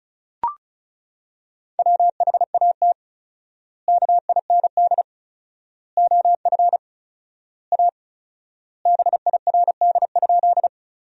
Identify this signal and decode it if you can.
Morse code, heard as a keyed tone, about 36 wpm, 700 Hz. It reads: WHAT KIND OF A BIRD?
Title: 4x Word-Spacing - Sentences from Top 400 Words - 35wpm